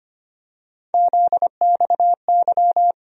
Morse code, heard as a keyed tone, about 25 wpm, 700 Hz. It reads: ZXY
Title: USA Call Sign Postfixes - 3 Letters 25wpm